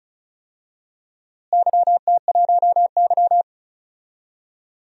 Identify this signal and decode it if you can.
Morse code, heard as a keyed tone, about 34 wpm, 700 Hz. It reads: YT1Y
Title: DXCC-100 Call Signs - 4 Characters - Repeated 2x - 35wpm